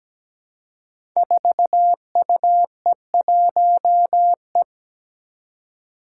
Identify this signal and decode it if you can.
Morse code, heard as a keyed tone, about 17 wpm, 700 Hz. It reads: 4UE1E
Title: Five Challenging Letter-Number Combinations 17wpm